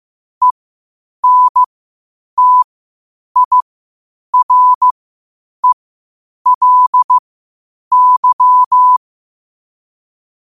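Morse code, heard as a keyed tone, about 15 wpm, spaced as for 8 wpm, 1000 Hz. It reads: ENTIRELY
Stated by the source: Morse code quotes at 8 words per minute from QOTD at 8 WPM.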